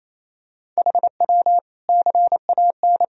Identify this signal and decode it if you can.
Morse code, heard as a keyed tone, about 28 wpm, 700 Hz. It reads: HW CAN
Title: Sentences from Top 500 Words and Abbreviations 28wpm